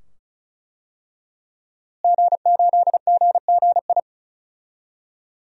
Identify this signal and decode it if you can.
Morse code, heard as a keyed tone, about 34 wpm, 700 Hz. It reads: G8GGI
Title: DXCC-100 Call Signs - 5 Characters - Repeated 2x - 35wpm